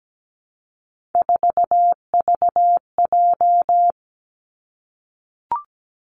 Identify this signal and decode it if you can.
Morse code, heard as a keyed tone, about 17 wpm, 700 Hz. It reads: 4VJ
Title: Three Challenging Letter-Number Combinations 17wpm